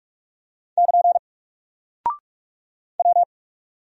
Morse code, heard as a keyed tone, about 43 wpm, 700 Hz. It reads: <KN> W
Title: Single Character (Letter:Number:Punctuation:Prosign) Alt 45wpm